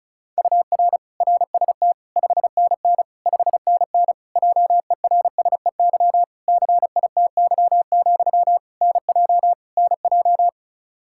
Morse code, heard as a keyed tone, about 36 wpm, 700 Hz. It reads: UR RST 5NN 5NN JERSEY CITY, NJ NJ
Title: Practice QSO - Exchange 1 - Beginning - 35wpm